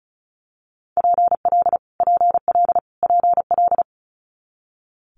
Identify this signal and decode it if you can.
Morse code, heard as a keyed tone, about 36 wpm, 700 Hz. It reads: PL PL PL
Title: Multi-letter Phonemes Repeated 3x 35wpm